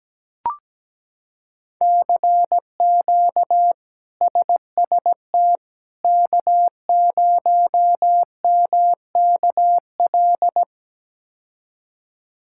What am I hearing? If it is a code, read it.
Morse code, heard as a keyed tone, about 17 wpm, 700 Hz. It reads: CQ SST K0MKL
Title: K1USN SST QSO Practice - 17wpm